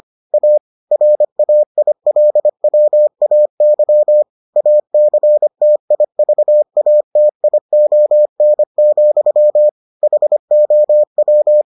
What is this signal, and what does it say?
Morse code, heard as a keyed tone, about 25 wpm, 600 Hz. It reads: A RAILWAY ACTIVATION, HOW